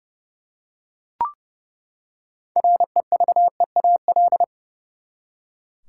Morse code, heard as a keyed tone, about 32 wpm, 700 Hz. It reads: REVEAL
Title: Six Letter Words - 30wpm